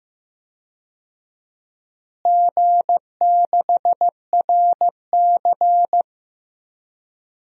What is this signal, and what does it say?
Morse code, heard as a keyed tone, about 15 wpm, 700 Hz. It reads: G6RC
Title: DXCC-100 Call Signs - 4 Characters - Repeated 2x - 15wpm